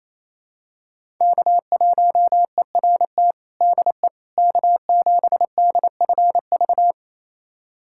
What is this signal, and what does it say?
Morse code, heard as a keyed tone, about 28 wpm, 700 Hz. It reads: K1ERT DE K7DFV
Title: USA Callsign DE Callsign - 5 Characters - 28wpm